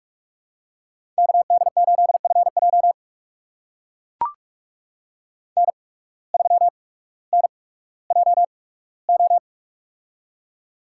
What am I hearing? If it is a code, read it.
Morse code, heard as a keyed tone, about 44 wpm, spaced as for 19 wpm, 700 Hz. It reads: KD8FJ N3NJO
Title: USA Call Signs - 5 Characters - 8x Character Spacing - 45wpm